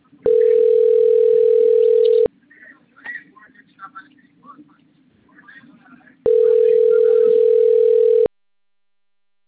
Ringback tone (North America), 440 + 480 Hz.